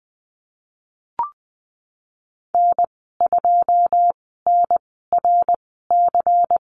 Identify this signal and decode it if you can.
Morse code, heard as a keyed tone, about 20 wpm, 700 Hz. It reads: N 2 N R C
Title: 154 Lesson 25g - Letter p - Call Signs 20wpm